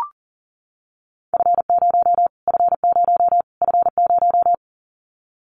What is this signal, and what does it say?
Morse code, heard as a keyed tone, about 40 wpm, 700 Hz. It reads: F0 F0 F0
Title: Callsign Prefixes for Top 25 DXCC - Repeated 3x - 40wpm